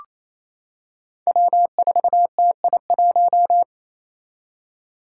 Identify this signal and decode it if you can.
Morse code, heard as a keyed tone, about 28 wpm, 700 Hz. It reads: W4TI1
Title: Five Challenging Letter-Number Combinations 28wpm